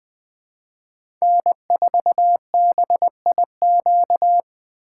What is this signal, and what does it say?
Morse code, heard as a keyed tone, about 21 wpm, 700 Hz. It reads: N4BIQ
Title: USA Call Signs - 5 Characters - 4x Character Spacing - 20wpm